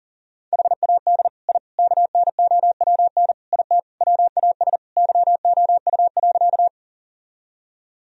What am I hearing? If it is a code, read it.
Morse code, heard as a keyed tone, about 40 wpm, 700 Hz. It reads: HAD I KNOWN IT WAS YOU.
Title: Sentences from Top 1000 Words - Review Part 3 - 40wpm